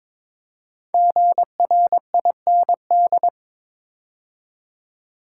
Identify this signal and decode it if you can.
Morse code, heard as a keyed tone, about 22 wpm, 690 Hz. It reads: GRIND
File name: 5 Letter Words - First 3 Letters Given - 22wpm